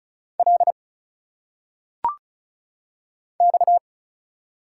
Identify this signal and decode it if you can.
Morse code, heard as a keyed tone, about 34 wpm, 700 Hz. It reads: L X
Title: Single Letters - Phonetic Pronunciation 35wpm